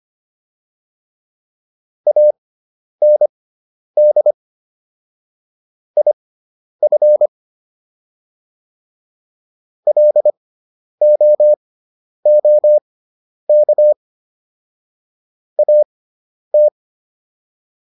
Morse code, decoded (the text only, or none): AND IF LOOK AT